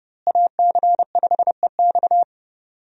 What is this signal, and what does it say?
Morse code, heard as a keyed tone, about 29 wpm, 700 Hz. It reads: AC5EX
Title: USA Call Signs - 5 Characters - Repeated 2x - 30wpm